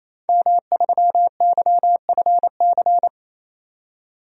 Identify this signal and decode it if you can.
Morse code, heard as a keyed tone, about 28 wpm, 700 Hz. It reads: M3YFC
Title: DXCC-100 Call Signs - 5 Characters - 28wpm